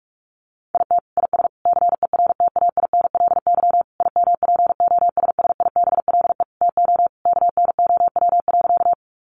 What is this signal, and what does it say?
Morse code, heard as a keyed tone, about 44 wpm, 700 Hz. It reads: IT IS CERTAINLY IMPOSSIBLE TO KNOW.
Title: Sentences from Top 1100 Words 45wpm